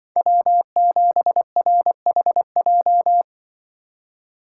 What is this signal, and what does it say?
Morse code, heard as a keyed tone, about 24 wpm, 700 Hz. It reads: W7RHJ
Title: Speed Racing - USA 1x3 Call Signs 20wpm